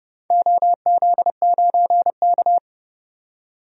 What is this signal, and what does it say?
Morse code, heard as a keyed tone, about 31 wpm, 700 Hz. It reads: OZ9K
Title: DXCC-100 Call Signs - 4 Characters - 30wpm